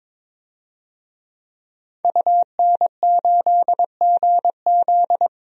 Morse code, heard as a keyed tone, about 22 wpm, 700 Hz. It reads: UN8GZ
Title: DXCC-100 Call Signs - 5 Characters - Repeated 3x - 22wpm